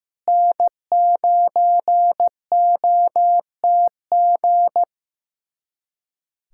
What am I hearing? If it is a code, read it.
Morse code, heard as a keyed tone, about 15 wpm, 700 Hz. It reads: N9OTG
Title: USA Callsign DE Callsign - 5 Characters - 15wpm